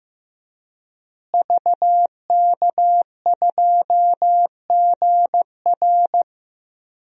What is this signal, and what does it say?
Morse code, heard as a keyed tone, about 15 wpm, 700 Hz. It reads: VK2GR
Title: DXCC-100 Call Signs - 5 Characters - Repeated 3x - 15wpm